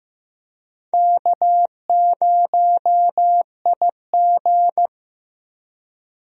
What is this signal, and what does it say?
Morse code, heard as a keyed tone, about 15 wpm, 700 Hz. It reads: K0IG
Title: USA Call Signs - 4 Characters - Repeated 2x - 15wpm